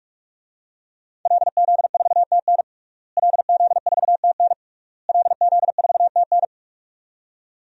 Morse code, heard as a keyed tone, about 46 wpm, 700 Hz. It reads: LZ4TN LZ4TN LZ4TN
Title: DXCC-100 Call Signs - 5 Characters - Repeated 3x - 45wpm